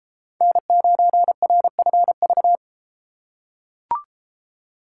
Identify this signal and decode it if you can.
Morse code, heard as a keyed tone, about 33 wpm, 700 Hz. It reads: N9RFV